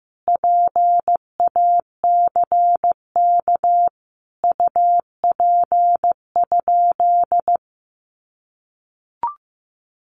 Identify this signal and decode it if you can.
Morse code, heard as a keyed tone, about 15 wpm, 700 Hz. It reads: PACK UP?
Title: Sentences from Top 1300 Words 15wpm